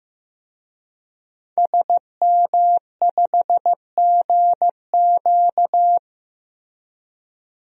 Morse code, heard as a keyed tone, about 15 wpm, 700 Hz. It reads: SM5GQ
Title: DXCC-100 Call Signs - 5 Characters - Repeated 3x - 15wpm